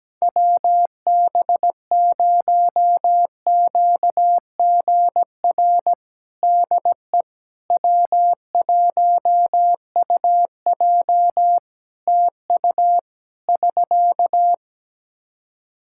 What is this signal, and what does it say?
Morse code, heard as a keyed tone, about 17 wpm, 700 Hz. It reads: WB0QGR DE W1UJ TU <SK>
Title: Practice QSO - Exchange 4 - Ending - 17wpm